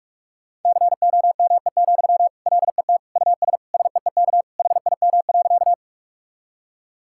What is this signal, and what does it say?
Morse code, heard as a keyed tone, about 45 wpm, 700 Hz. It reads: COME, LET US SEEK HIM.
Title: Sentences from Top 1600 Words 45wpm